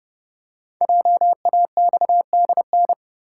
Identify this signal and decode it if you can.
Morse code, heard as a keyed tone, about 29 wpm, 700 Hz. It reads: JAXDN
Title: Code Groups - 5 Letters - 30wpm